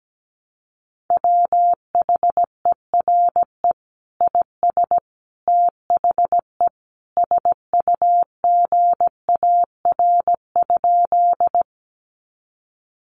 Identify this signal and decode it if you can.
Morse code, heard as a keyed tone, about 17 wpm, 700 Hz. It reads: WHERE IS THE SUGAR?
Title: Sentences from Top 1200 Words 17wpm